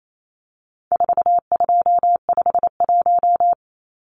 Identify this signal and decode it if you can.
Morse code, heard as a keyed tone, about 28 wpm, 700 Hz. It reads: 4251
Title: Numbers - 4-Digit 28wpm